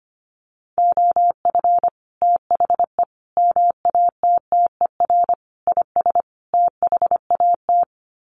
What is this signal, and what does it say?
Morse code, heard as a keyed tone, about 25 wpm, 700 Hz. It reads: OF THE MATTER IS THAT
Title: Sets of 5 Words Encore 25wpm